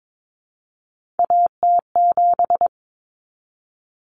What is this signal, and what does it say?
Morse code, heard as a keyed tone, about 22 wpm, 700 Hz. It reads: AT7
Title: Common Cut Numbers - 3-Digit 22wpm